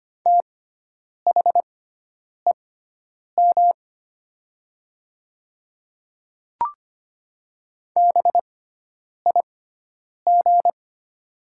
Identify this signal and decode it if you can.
Morse code, heard as a keyed tone, about 25 wpm, spaced as for 9 wpm, 700 Hz. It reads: THEM BIG
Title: 6x Character Spacing - Top 300 Words - 25wpm